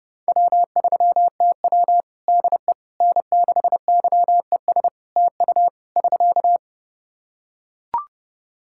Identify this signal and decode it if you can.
Morse code, heard as a keyed tone, about 30 wpm, 700 Hz. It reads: W3TW DE N6YES TU <SK>
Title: Practice QSO - Exchange 4 - Ending - 30wpm